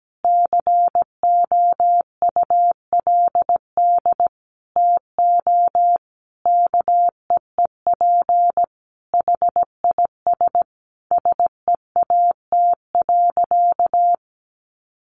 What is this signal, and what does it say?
Morse code, heard as a keyed tone, about 17 wpm, 700 Hz. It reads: COULD TO KEEP HIS SEAT.
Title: Sentences from Top 1500 Words 17wpm